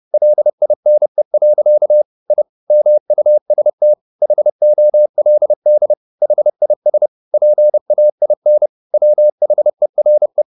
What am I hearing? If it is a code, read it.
Morse code, heard as a keyed tone, about 30 wpm, 600 Hz. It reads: LINE. I MUST HOLD HIS PAIN WHERE